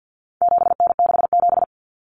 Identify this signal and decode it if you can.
Morse code, heard as a keyed tone, about 50 wpm, 700 Hz. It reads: 7N67
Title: Common Cut Numbers - 4-Digit 50wpm